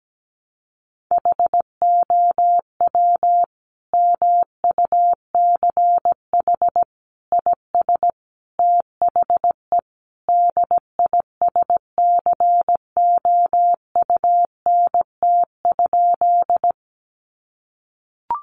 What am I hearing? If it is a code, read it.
Morse code, heard as a keyed tone, about 17 wpm, 700 Hz. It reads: HOW MUCH IS THE DISCOUNT?
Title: Sentences from Top 1200 Words 17wpm